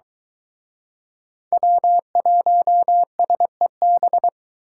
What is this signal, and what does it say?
Morse code, heard as a keyed tone, about 24 wpm, 700 Hz. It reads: W1SEB